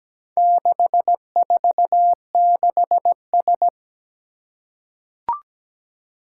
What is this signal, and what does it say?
Morse code, heard as a keyed tone, about 17 wpm, 700 Hz. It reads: 646S
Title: Four Challenging Letter-Number Combinations 17wpm